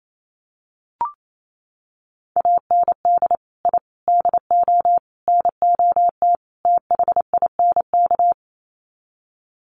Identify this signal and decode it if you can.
Morse code, heard as a keyed tone, about 28 wpm, 700 Hz. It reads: AND I DO NOT THINK
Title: Sets of 5 Words 28wpm